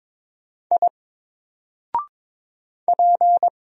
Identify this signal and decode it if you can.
Morse code, heard as a keyed tone, about 22 wpm, 700 Hz. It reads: I P
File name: Single-Letter-Number-22wpm